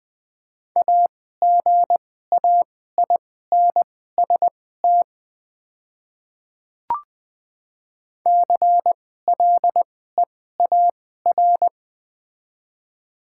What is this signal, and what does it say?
Morse code, heard as a keyed tone, about 20 wpm, spaced as for 15 wpm, 700 Hz. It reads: AGAINST CLEAR
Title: 2x Character Spacing - Top 500 Words - 20wpm